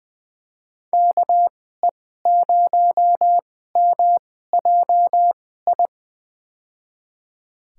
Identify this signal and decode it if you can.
Morse code, heard as a keyed tone, about 21 wpm, spaced as for 15 wpm, 700 Hz. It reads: KE0MJI
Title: USA Call Signs - 6 Characters - 2x Character Spacing - 20wpm